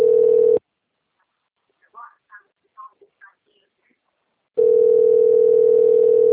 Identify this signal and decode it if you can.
Ringback tone (North America), 440 + 480 Hz.